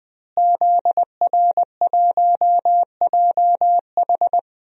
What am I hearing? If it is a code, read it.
Morse code, heard as a keyed tone, about 20 wpm, 700 Hz. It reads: ZR1JH
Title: DXCC-100 Call Signs - 5 Characters - Repeated 2x - 20wpm